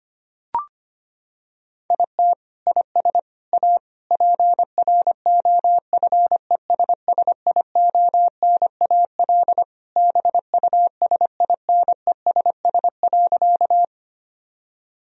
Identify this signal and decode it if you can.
Morse code, heard as a keyed tone, about 25 wpm, 700 Hz. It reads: IT IS A PROFESSIONAL BUSINESS.
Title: Sentences from Top 1000 Words - Review Part 4 - 25wpm